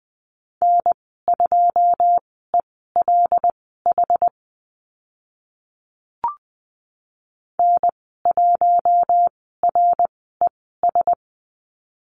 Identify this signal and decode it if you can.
Morse code, heard as a keyed tone, about 20 wpm, spaced as for 14 wpm, 700 Hz. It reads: N2ELH N1RES
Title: 082 Lesson 14g - Number 2 - Call Signs 20wpm